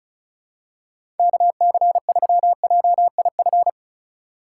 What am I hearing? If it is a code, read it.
Morse code, heard as a keyed tone, about 36 wpm, 700 Hz. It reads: KC3JIF
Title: USA Call Signs - 6 Characters - 3x Character Spacing - 35wpm